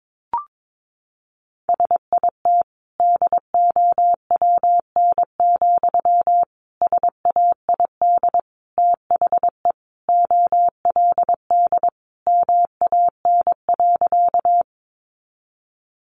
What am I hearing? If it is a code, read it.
Morse code, heard as a keyed tone, about 22 wpm, 700 Hz. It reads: SIT DOWN, SAID THE OLD MAN.
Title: Sentences from Top 500 Words 22wpm